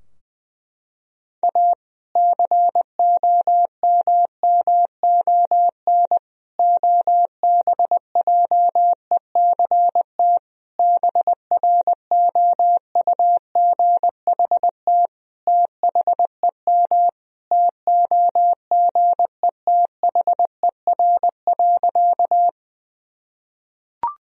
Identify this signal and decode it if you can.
Morse code, heard as a keyed tone, about 20 wpm, 700 Hz. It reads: A COMMON OBJECT BROUGHT THEM TOGETHER.
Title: Sentences from Top 1000 Words - Review Part 3 - 20wpm